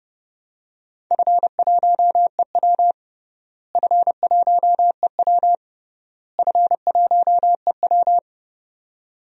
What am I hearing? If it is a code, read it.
Morse code, heard as a keyed tone, about 28 wpm, 700 Hz. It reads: F1EW F1EW F1EW
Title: DXCC-100 Call Signs - 4 Characters - Repeated 3x - 30wpm